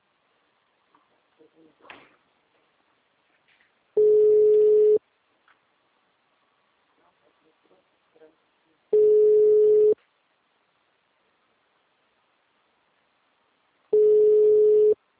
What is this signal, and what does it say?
Ringback tone (Europe, 425 Hz), with speech.
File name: out- -2001- - - .86